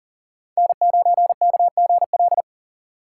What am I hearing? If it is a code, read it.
Morse code, heard as a keyed tone, about 37 wpm, 700 Hz. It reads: N9KGL
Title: USA Call Signs - 2x Character Spacing - 40wpm